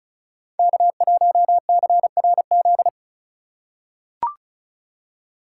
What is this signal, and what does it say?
Morse code, heard as a keyed tone, about 34 wpm, 700 Hz. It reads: K1CRZ
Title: USA Call Signs - 3x Character Spacing - 35wpm